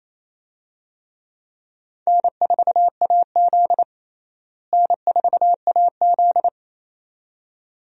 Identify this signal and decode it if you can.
Morse code, heard as a keyed tone, about 28 wpm, 700 Hz. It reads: N4AZ N4AZ
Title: USA Call Signs - 4 Characters - Repeated 2x - 28wpm